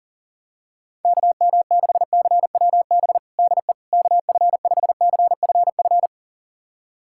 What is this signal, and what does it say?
Morse code, heard as a keyed tone, about 40 wpm, 700 Hz. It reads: KM6CWB DE KF5CFF
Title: USA Callsign DE Callsign 40wpm